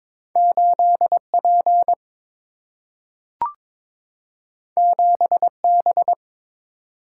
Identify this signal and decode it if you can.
Morse code, heard as a keyed tone, about 22 wpm, 700 Hz. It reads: 8P 7B
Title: Single-Letter Single-Number 22wpm